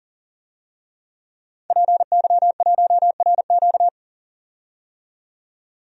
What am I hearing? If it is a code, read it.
Morse code, heard as a keyed tone, about 39 wpm, 700 Hz. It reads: PY1RQ
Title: DXCC-100 Call Signs - 5 Characters - 40wpm